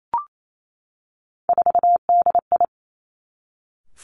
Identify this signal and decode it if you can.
Morse code, heard as a keyed tone, about 28 wpm, 700 Hz. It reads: 4DI
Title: Three Challenging Letter-Number Combinations 28wpm